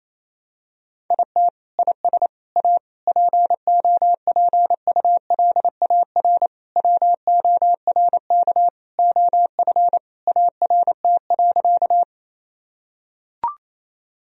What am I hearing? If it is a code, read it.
Morse code, heard as a keyed tone, about 28 wpm, 700 Hz. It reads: IT IS A POPULAR WORK OF ART.
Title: Sentences from Top 1000 Words - Review Part 3 - 28wpm